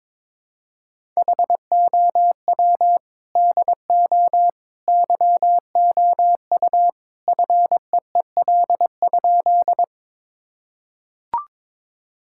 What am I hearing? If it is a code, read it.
Morse code, heard as a keyed tone, about 22 wpm, 700 Hz. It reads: HOW DO YOU FEEL?
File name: practice / 22wpm / Sentences with 3-Word Phrases - 22wpm